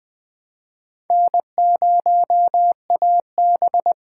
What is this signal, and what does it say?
Morse code, heard as a keyed tone, about 21 wpm, 700 Hz. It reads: N0AB